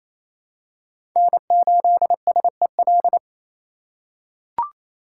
Morse code, heard as a keyed tone, about 28 wpm, 700 Hz. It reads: N8SEL
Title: USA Call Signs - 5 Characters - 6x Character Spacing - 28wpm